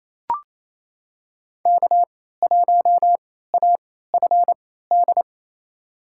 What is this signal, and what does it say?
Morse code, heard as a keyed tone, about 28 wpm, spaced as for 16 wpm, 700 Hz. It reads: K1AFD
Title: USA Call Signs - 3x Character Spacing - 28wpm